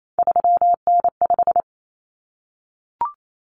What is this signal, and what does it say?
Morse code, heard as a keyed tone, about 28 wpm, 700 Hz. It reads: 3N5
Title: Common Cut Numbers - 3-Digit 28wpm